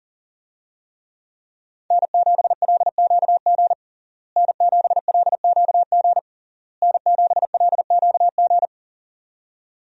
Morse code, heard as a keyed tone, about 37 wpm, 700 Hz. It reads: N7LQG N7LQG N7LQG
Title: USA Call Signs - 5 Characters - Repeated 3x - 40wpm